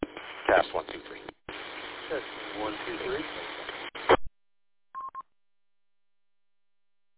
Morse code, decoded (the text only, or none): none